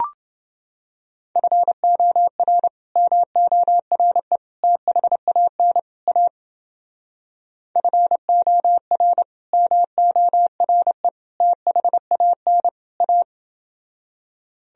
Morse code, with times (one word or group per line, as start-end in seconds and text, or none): 1.4-2.7 FOR
3.0-4.4 MORE
4.6-5.8 THAN
6.1-6.3 A
7.8-9.2 FOR
9.5-11.1 MORE
11.4-12.7 THAN
13.0-13.2 A